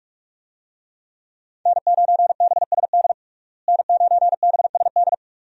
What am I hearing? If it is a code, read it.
Morse code, heard as a keyed tone, about 46 wpm, 700 Hz. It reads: N9BSD N9BSD
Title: USA Call Signs - 5 Characters - Repeated 2x - 45wpm